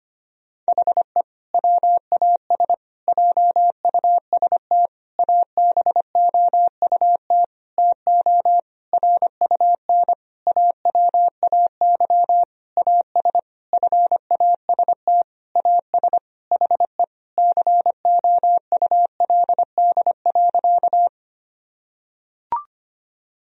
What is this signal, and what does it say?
Morse code, heard as a keyed tone, about 25 wpm, 700 Hz. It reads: HE WAS JUST ABOUT TO RUN AWAY AS FAST AS HE COULD.
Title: Sentences from Top 1000 Words - Review Part 2 - 25wpm